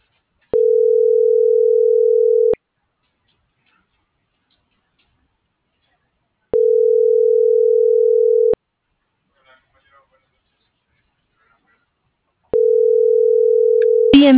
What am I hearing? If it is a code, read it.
Ringback tone (North America), 440 + 480 Hz.